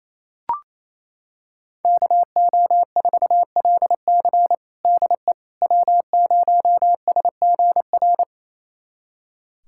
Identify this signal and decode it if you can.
Morse code, heard as a keyed tone, about 28 wpm, 690 Hz. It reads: KO4LC DE W0SGR
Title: USA Callsign DE Callsign - 5 Characters - 28wpm